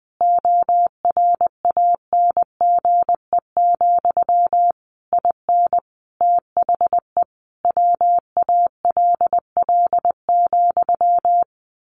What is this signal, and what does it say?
Morse code, heard as a keyed tone, about 20 wpm, 700 Hz. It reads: ORANGE, IN THE WALL,
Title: A Sound Of Thunder 20wpm